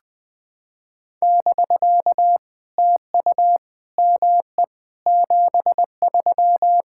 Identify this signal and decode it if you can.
Morse code, heard as a keyed tone, about 20 wpm, 700 Hz. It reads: <BK> TU ME 73
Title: POTA QSO Practice - 20wpm